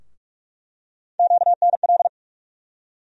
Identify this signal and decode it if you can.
Morse code, heard as a keyed tone, about 46 wpm, 700 Hz. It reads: QNL
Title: USA Call Sign Postfixes - 3 Letters 45wpm